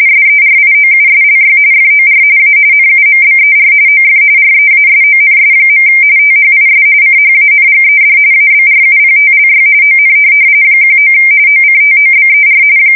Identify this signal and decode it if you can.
RTTY, about 45.45 baud. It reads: SERIOUSLY FOR A NUMBER OF REASONS. THIS CHANGED ABRUPTLY ON 19 OCTOBER
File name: fsk170 / USS-Missouri